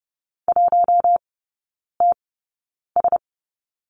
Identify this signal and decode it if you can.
Morse code, heard as a keyed tone, about 30 wpm, spaced as for 9 wpm, 700 Hz. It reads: 1TS
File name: Lesson 19e - Letter w - Three Characters 30wpm